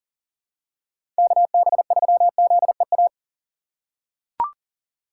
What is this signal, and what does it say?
Morse code, heard as a keyed tone, about 38 wpm, 700 Hz. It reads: KB3ZEA
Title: USA Call Signs - 6 Characters - 4x Character Spacing - 40wpm